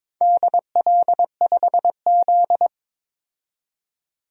Morse code, heard as a keyed tone, about 22 wpm, 700 Hz. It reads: DL5Z
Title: DXCC-100 Call Signs - 4 Characters - 22wpm